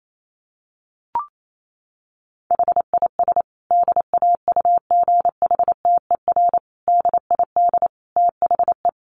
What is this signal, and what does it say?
Morse code, heard as a keyed tone, about 28 wpm, 700 Hz. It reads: HIS DAUGHTER DID THE
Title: Sentences from Top 1500 Words 28wpm